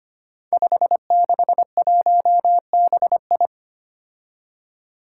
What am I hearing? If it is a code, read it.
Morse code, heard as a keyed tone, about 25 wpm, 700 Hz. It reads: 561BI
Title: Five Challenging Letter-Number Combinations 25wpm